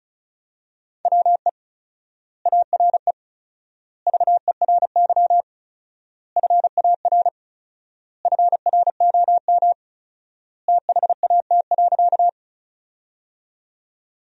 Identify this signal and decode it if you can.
Morse code, heard as a keyed tone, about 34 wpm, 700 Hz. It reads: WE ARE VERY FAR FROM THAT.
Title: 4x Word-Spacing - Sentences from Top 300 Words - 35wpm